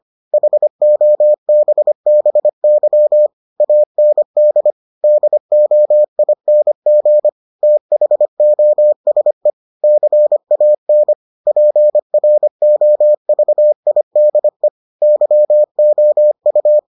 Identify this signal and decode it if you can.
Morse code, heard as a keyed tone, about 25 wpm, 600 Hz. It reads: HOBBY AND DOING THOSE CAN PROVIDE YOU